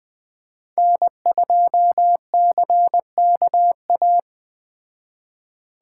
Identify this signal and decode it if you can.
Morse code, heard as a keyed tone, about 20 wpm, 700 Hz. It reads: N2CKA
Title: Speed Racing - USA Call Signs 17wpm